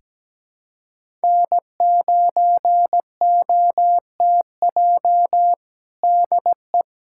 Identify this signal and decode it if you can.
Morse code, heard as a keyed tone, about 17 wpm, 700 Hz. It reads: N9OTJ DE
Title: USA Callsign DE Callsign - 5 Characters - 17wpm